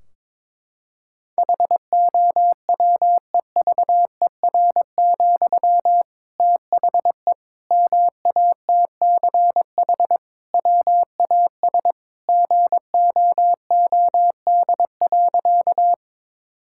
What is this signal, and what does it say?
Morse code, heard as a keyed tone, about 21 wpm, 700 Hz. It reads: HOWEVER, THE MATCH WAS GOOD.
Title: Sentences from Top 1600 Words 22wpm